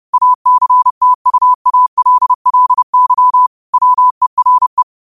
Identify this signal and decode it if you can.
Morse code, heard as a keyed tone, about 30 wpm, 1000 Hz. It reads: ACTUALLY WERE